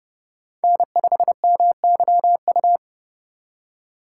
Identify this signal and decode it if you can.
Morse code, heard as a keyed tone, about 30 wpm, 700 Hz. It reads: N5MYU
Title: USA Call Signs - 5 Characters - 4x Character Spacing - 30wpm